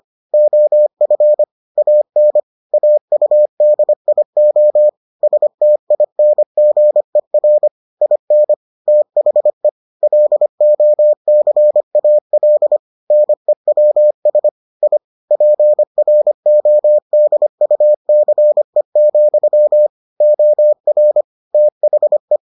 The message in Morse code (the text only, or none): OF AN AUDIO STINGER IN THE LOCAL NEWS I PRODUCE, OR THE